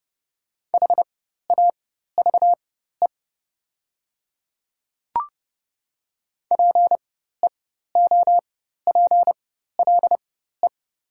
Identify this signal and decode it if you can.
Morse code, heard as a keyed tone, about 30 wpm, spaced as for 14 wpm, 700 Hz. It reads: HAVE PEOPLE
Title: 4x Character Spacing - Top 100 Words - 30wpm